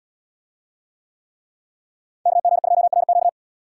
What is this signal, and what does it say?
Morse code, heard as a keyed tone, about 71 wpm, 700 Hz. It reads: KG2RQ